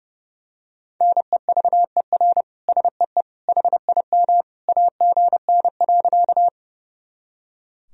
Morse code, heard as a keyed tone, about 32 wpm, 700 Hz. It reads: NEVER SEE HIM AGN.
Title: Sentences from Top 300 Words and Abbreviations 30wpm